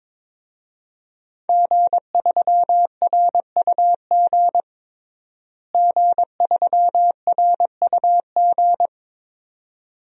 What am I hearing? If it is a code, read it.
Morse code, heard as a keyed tone, about 22 wpm, 700 Hz. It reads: G3RUG G3RUG
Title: DXCC-100 Call Signs - 5 Characters - Repeated 2x - 22wpm